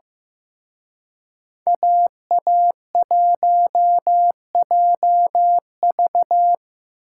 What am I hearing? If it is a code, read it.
Morse code, heard as a keyed tone, about 18 wpm, 700 Hz. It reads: AA1JV
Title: Speed Racing - USA 2x2 Call Signs 15wpm